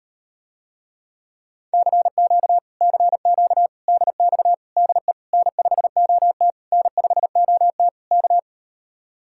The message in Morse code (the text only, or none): CQ CQ DX DE N5OT N5OT K